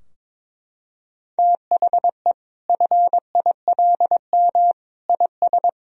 Morse code, heard as a keyed tone, about 22 wpm, 700 Hz. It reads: THE FILM IS
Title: Sentences from Top 1000 Words - Review Part 3 - 22wpm